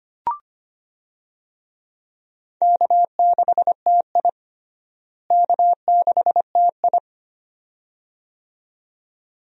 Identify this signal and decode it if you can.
Morse code, heard as a keyed tone, about 25 wpm, 700 Hz. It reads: K6TI K6TI
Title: USA Call Signs - 4 Characters - Repeated 2x - 25wpm